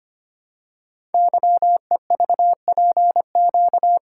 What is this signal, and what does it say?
Morse code, heard as a keyed tone, about 25 wpm, 700 Hz. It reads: YEVPQ
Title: Code Groups - 5 Letters - 25wpm